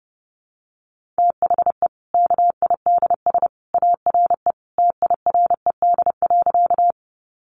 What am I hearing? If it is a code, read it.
Morse code, heard as a keyed tone, about 30 wpm, 700 Hz. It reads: THE KIDS ARE TIRED.
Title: Easy Phrases 30wpm